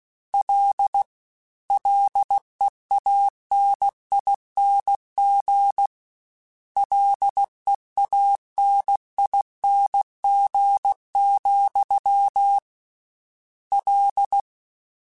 Morse code, heard as a keyed tone, about 16 wpm, 800 Hz. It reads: L LEANING LEANING, L
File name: Combinations / Combinations-mp3s / Combinations@15